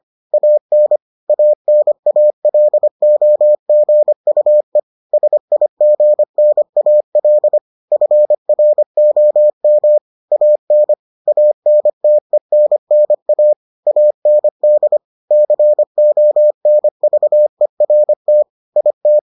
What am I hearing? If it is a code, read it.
Morse code, heard as a keyed tone, about 25 wpm, 600 Hz. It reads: AN ANALOGUE SIGNAL FROM AN ANTENNA AND CONVERT IT